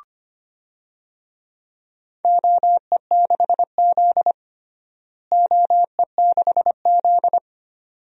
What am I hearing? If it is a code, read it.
Morse code, heard as a keyed tone, about 25 wpm, 700 Hz. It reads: OE6Z OE6Z
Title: DXCC-100 Call Signs - 4 Characters - Repeated 2x - 25wpm